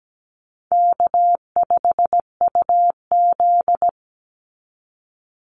Morse code, heard as a keyed tone, about 17 wpm, 700 Hz. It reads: K5UZ
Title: Easy Call Signs - 17wpm